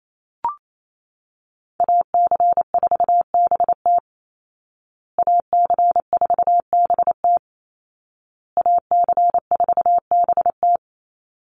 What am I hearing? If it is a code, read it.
Morse code, heard as a keyed tone, about 28 wpm, 700 Hz. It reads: AC4BT AC4BT AC4BT
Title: Call Signs Repeated 2x - 28wpm